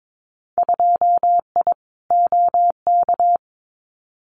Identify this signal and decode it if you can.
Morse code, heard as a keyed tone, about 22 wpm, 700 Hz. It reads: 2I OK
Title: Winter Field Day Exchanges Encore 22wpm